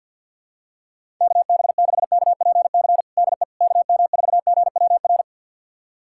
Morse code, heard as a keyed tone, about 51 wpm, 700 Hz. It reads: KB6KPC DE KM4GWR
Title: USA Callsign DE Callsign - 6 Characters - 50wpm